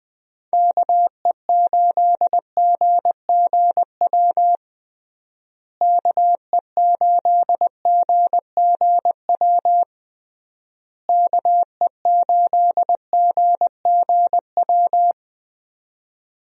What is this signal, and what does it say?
Morse code, heard as a keyed tone, about 21 wpm, 700 Hz. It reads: KE8GGW KE8GGW KE8GGW
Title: USA Call Signs - 6 Characters - Repeated 3x - 20wpm